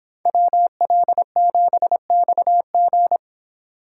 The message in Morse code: WL7XG

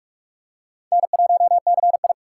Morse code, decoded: N1CI

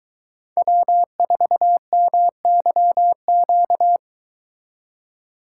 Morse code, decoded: W4MYQ

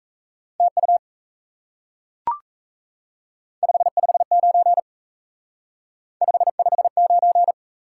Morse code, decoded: TU 559 559

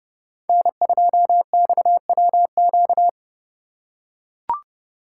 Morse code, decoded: N2XWQ

